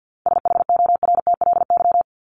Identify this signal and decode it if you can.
Morse code, heard as a keyed tone, about 50 wpm, 700 Hz. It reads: SHORTLY
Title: LY Suffix - Top 100 Words with Base Words 50wpm